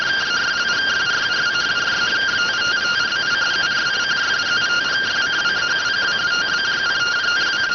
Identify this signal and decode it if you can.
Two-tone teleprinter signal, about 170 Hz shift.